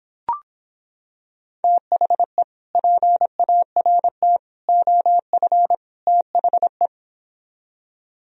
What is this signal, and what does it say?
Morse code, heard as a keyed tone, about 26 wpm, 700 Hz. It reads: THE PART OF THE